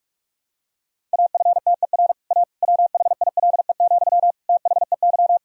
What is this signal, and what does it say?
Morse code, heard as a keyed tone, about 46 wpm, 700 Hz. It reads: AFTER A WHILE, THEY
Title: Sentences from Top 1000 Words - Review Part 1 - 45wpm